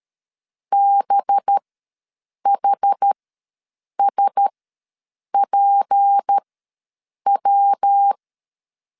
Morse code, heard as a keyed tone, about 13 wpm, spaced as for 7 wpm, 800 Hz. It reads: BHSPW
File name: LEZIONI / 6 PKX0 / 6 PK ws 12wpm 3116